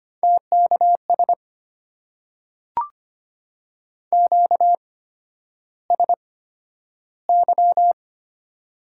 Morse code, heard as a keyed tone, about 25 wpm, spaced as for 11 wpm, 700 Hz. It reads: TKS QSY
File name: Top 90 QSO Elements - 8x Character Spacing - 25wpm